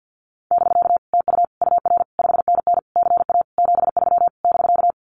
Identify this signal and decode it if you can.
Morse code, heard as a keyed tone, about 50 wpm, 700 Hz. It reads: <BK> TU UR 5NN CA 73 <BK>
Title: Common QSO Phrases Encore 50wpm